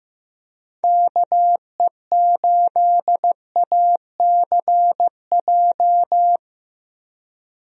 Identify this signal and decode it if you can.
Morse code, heard as a keyed tone, about 15 wpm, 700 Hz. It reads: KE8ACJ
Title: USA Call Signs - 6 Characters - 15wpm